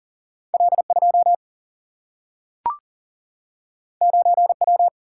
Morse code, heard as a keyed tone, about 39 wpm, 700 Hz. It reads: L 2 9 W
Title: Single-Letter Single-Number 40wpm